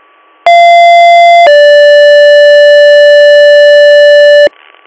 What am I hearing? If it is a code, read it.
Two-tone page, heard as a one-second tone, then a three-second tone, about 690 then 580 Hz, with noise. It is heard from 0.5 to 4.5 s.